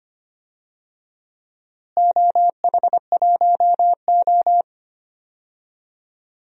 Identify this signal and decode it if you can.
Morse code, heard as a keyed tone, about 25 wpm, 700 Hz. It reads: OH1O
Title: DXCC-100 Call Signs - 4 Characters - Repeated 2x - 25wpm